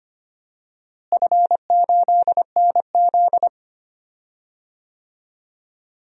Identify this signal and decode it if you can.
Morse code, heard as a keyed tone, about 25 wpm, 700 Hz. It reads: F8NZ
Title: DXCC-100 Call Signs - 4 Characters - Repeated 2x - 25wpm